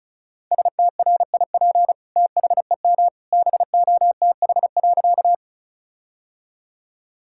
Morse code, heard as a keyed tone, about 34 wpm, 700 Hz. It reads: STRIP THEM BOTH.
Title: Sentences from Top 1600 Words 35wpm